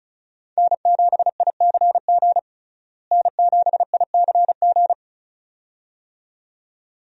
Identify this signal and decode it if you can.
Morse code, heard as a keyed tone, about 34 wpm, 700 Hz. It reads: N7ICG N7ICG
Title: USA Call Signs - 5 Characters - Repeated 2x - 35wpm